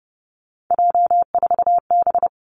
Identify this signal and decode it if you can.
Morse code, heard as a keyed tone, about 30 wpm, 700 Hz. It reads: J4B
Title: Three Challenging Letter-Number Combinations 30wpm